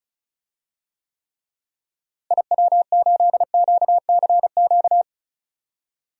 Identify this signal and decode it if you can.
Morse code, heard as a keyed tone, about 36 wpm, 700 Hz. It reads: IW8QCQ
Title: DXCC-100 Call Signs - 6 Characters - Repeated 3x - 35wpm